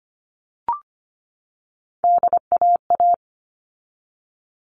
Morse code, heard as a keyed tone, about 25 wpm, 700 Hz. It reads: DAA
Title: All Cut Numbers - 3-Digit 25wpm